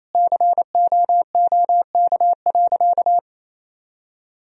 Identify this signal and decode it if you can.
Morse code, heard as a keyed tone, about 28 wpm, 700 Hz. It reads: COOK.
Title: Sentences from Top 1000 Words - Review Part 4 - 28wpm